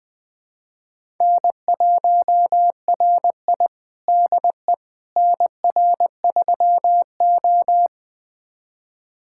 Morse code, heard as a keyed tone, about 20 wpm, 700 Hz. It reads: N1RI DE NR3O
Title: USA Callsign DE Callsign - 4 Characters - 20wpm